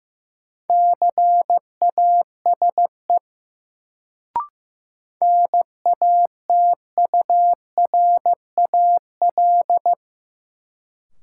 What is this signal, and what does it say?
Morse code, heard as a keyed tone, about 15 wpm, 700 Hz. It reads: CASE NATURAL
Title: Sentences from Top 700 Words - New Word Drill - 15wpm